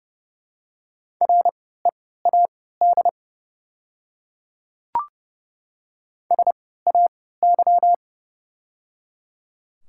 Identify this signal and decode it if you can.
Morse code, heard as a keyed tone, about 30 wpm, spaced as for 17 wpm, 690 Hz. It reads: READ SAY
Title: 3x Character Spacing - Top 200 Words - 30wpm